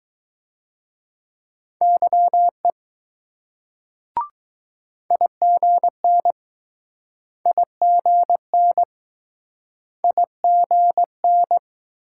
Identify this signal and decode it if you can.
Morse code, heard as a keyed tone, about 20 wpm, 700 Hz. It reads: YE IGN IGN IGN